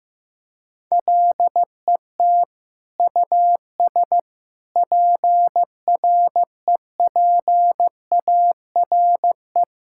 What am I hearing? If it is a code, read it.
Morse code, heard as a keyed tone, about 15 wpm, 700 Hz. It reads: LET US PREPARE
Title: Sentences from Top 1000 Words - Review Part 6 - 15wpm